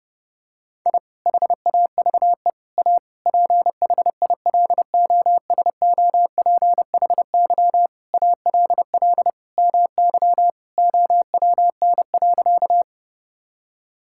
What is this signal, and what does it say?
Morse code, heard as a keyed tone, about 32 wpm, 700 Hz. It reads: I HAVE A PHILOSOPHY ALL MY OWN.
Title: Sentences from Top 1000 Words - Review Part 6 - 30wpm